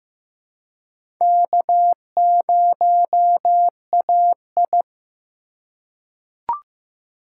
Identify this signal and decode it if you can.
Morse code, heard as a keyed tone, about 15 wpm, 700 Hz. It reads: K0AI
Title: USA Call Signs - 3x Character Spacing - 15wpm